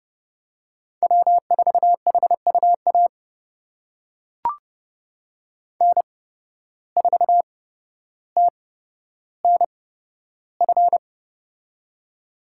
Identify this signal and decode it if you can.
Morse code, heard as a keyed tone, about 30 wpm, spaced as for 13 wpm, 700 Hz. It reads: W4HUA N4TNF
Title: USA Call Signs - 5 Characters - 8x Character Spacing - 30wpm